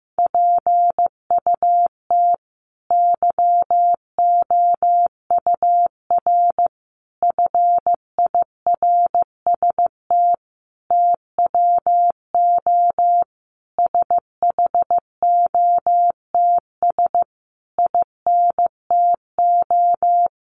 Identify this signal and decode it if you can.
Morse code, heard as a keyed tone, about 15 wpm, 700 Hz. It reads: PUT YOUR FIRST TWO SHOTS INTO
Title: A Sound Of Thunder 15wpm